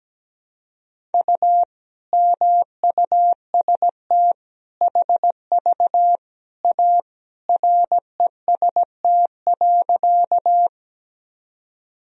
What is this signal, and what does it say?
Morse code, heard as a keyed tone, about 17 wpm, 700 Hz. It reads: U MUST HV A REST.
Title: Sentences from Top 500 Words and Abbreviations 17wpm